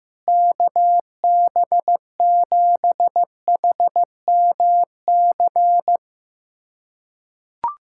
Morse code, heard as a keyed tone, about 15 wpm, 700 Hz. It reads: KB7HMC
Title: USA Callsign DE Callsign - 6 Characters - 15wpm